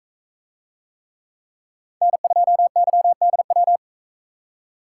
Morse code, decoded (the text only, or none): N2YDW